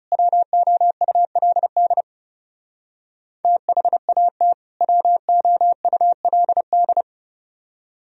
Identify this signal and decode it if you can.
Morse code, heard as a keyed tone, about 37 wpm, 700 Hz. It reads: WOULD THAT WOULD
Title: Speed-Racing - Sets of 2 Words 30wpm